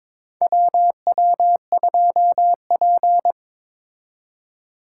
Morse code, heard as a keyed tone, about 22 wpm, 700 Hz. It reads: WW2P
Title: K1USN SST QSO Practice - 22wpm